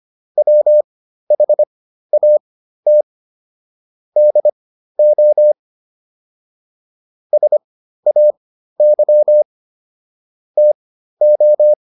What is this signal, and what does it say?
Morse code, heard as a keyed tone, about 25 wpm, spaced as for 13 wpm, 600 Hz. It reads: WHAT DO SAY TO